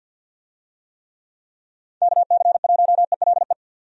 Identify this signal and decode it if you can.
Morse code, heard as a keyed tone, about 47 wpm, 700 Hz. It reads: KC1ELE
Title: USA Call Signs - 6 Characters - 50wpm